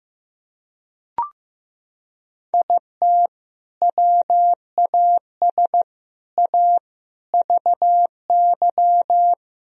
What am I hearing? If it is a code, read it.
Morse code, heard as a keyed tone, about 15 wpm, 700 Hz. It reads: IT WAS A VY
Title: Sentences from Top 500 Words and Abbreviations 15wpm